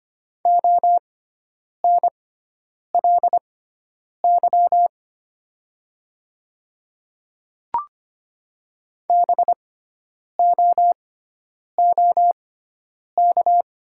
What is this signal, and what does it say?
Morse code, heard as a keyed tone, about 25 wpm, spaced as for 9 wpm, 700 Hz. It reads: ONLY BOOK
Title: 6x Character Spacing - Top 300 Words - 25wpm